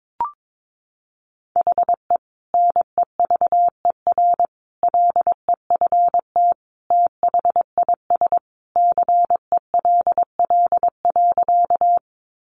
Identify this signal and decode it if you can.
Morse code, heard as a keyed tone, about 22 wpm, 700 Hz. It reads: HE NEVER LEFT THIS CELL.
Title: Sentences from Top 1100 Words 22wpm